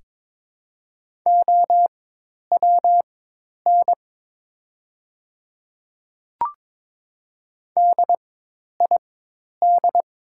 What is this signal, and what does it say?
Morse code, heard as a keyed tone, about 23 wpm, spaced as for 10 wpm, 700 Hz. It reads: OWN DID
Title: 4x Character Spacing - Top 300 Words - 22wpm